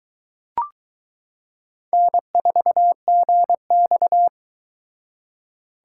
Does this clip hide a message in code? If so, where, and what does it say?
Yes: Morse code from 1.9 to 4.3 s, heard as a keyed tone, about 21 wpm, 700 Hz. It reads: N4GX